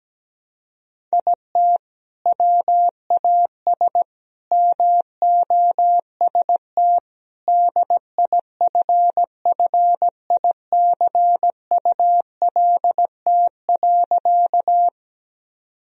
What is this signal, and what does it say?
Morse code, heard as a keyed tone, about 17 wpm, 700 Hz. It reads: IT WAS MOST DIFFICULT.
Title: Sentences from Top 1000 Words - Review Part 1 - 17wpm